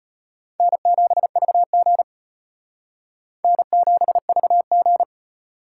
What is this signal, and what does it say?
Morse code, heard as a keyed tone, about 32 wpm, 700 Hz. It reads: N7VG N7VG